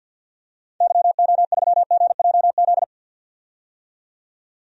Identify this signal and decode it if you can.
Morse code, heard as a keyed tone, about 46 wpm, 700 Hz. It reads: YO3GJZ
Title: DXCC-100 Call Signs - 6 Characters - 50wpm